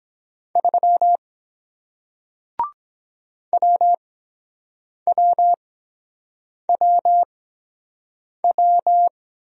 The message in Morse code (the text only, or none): 3 WWWW